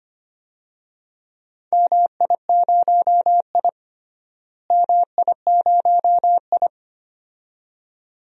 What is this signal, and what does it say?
Morse code, heard as a keyed tone, about 25 wpm, 700 Hz. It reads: MI0I MI0I
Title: DXCC-100 Call Signs - 4 Characters - Repeated 2x - 25wpm